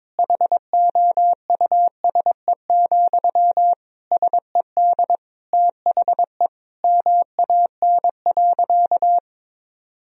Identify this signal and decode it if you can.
Morse code, heard as a keyed tone, about 22 wpm, 700 Hz. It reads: HOUSE, SED THE MAN.
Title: Sentences from Top 400 Words and Abbreviations 22wpm